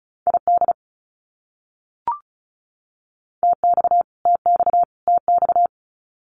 Morse code, heard as a keyed tone, about 34 wpm, 700 Hz. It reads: ID TX TX TX
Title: US States Repeated 3x 35wpm